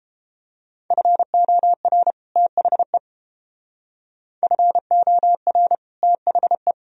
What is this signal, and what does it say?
Morse code, heard as a keyed tone, about 29 wpm, 700 Hz. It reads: FOR THE FOR THE